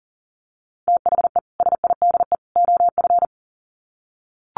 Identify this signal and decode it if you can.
Morse code, heard as a keyed tone, about 40 wpm, 700 Hz. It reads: THE SIDE OF
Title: Sets of 3 Words 40wpm